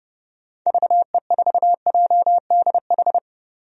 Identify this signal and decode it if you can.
Morse code, heard as a keyed tone, about 32 wpm, 700 Hz. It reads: VE4JDH
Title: DXCC-100 Call Signs - 6 Characters - Repeated 2x - 30wpm